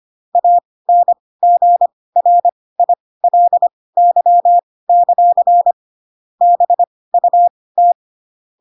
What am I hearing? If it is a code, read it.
Morse code, heard as a keyed tone, about 25 wpm, spaced as for 18 wpm, 700 Hz. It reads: ANGRILY; BUT
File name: Pinocchio ch3 12 25 ik1hge